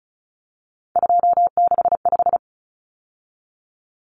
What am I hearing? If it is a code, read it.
Morse code, heard as a keyed tone, about 34 wpm, 700 Hz. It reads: 265
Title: Numbers - 3-Digit 35wpm